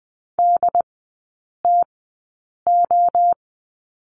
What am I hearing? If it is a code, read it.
Morse code, heard as a keyed tone, about 20 wpm, spaced as for 8 wpm, 690 Hz. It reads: DTO
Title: 081 Lesson 14e - Number 2 - Three Characters 20wpm